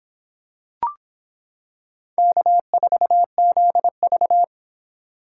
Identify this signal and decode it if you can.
Morse code, heard as a keyed tone, about 26 wpm, 700 Hz. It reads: K4ZV